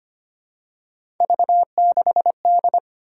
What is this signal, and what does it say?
Morse code, heard as a keyed tone, about 25 wpm, 700 Hz. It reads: V6D
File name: Three Letter-Number Combinations - 25wpm